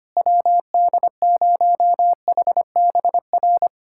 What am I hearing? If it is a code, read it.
Morse code, heard as a keyed tone, about 25 wpm, 700 Hz. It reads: WD0HBR
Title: USA Callsign DE Callsign - 6 Characters - 25wpm